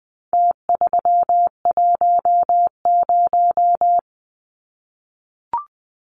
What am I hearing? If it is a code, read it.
Morse code, heard as a keyed tone, about 20 wpm, 700 Hz. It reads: T310
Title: Numbers - Contest Style - 4-Digit 20wpm